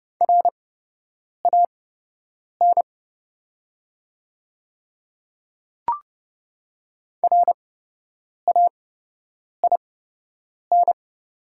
Morse code, heard as a keyed tone, about 30 wpm, spaced as for 8 wpm, 700 Hz. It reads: RAN RAIN
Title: 8x Character Spacing - Top 500 Words - 30wpm